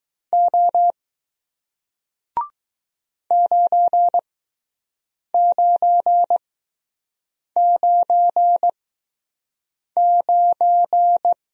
Morse code, heard as a keyed tone, about 20 wpm, spaced as for 6 wpm, 700 Hz. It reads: O 9999